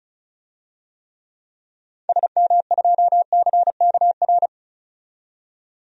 Morse code, heard as a keyed tone, about 36 wpm, 700 Hz. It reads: SM2CKR
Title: DXCC-100 Call Signs - 6 Characters - 35wpm